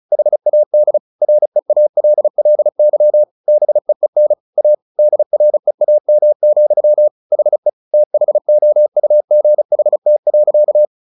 Morse code, heard as a keyed tone, about 34 wpm, 600 Hz. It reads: HAD REALLY BEEN A DREAM, HE THOUGHT.
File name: the old man and the sea 2 35wpm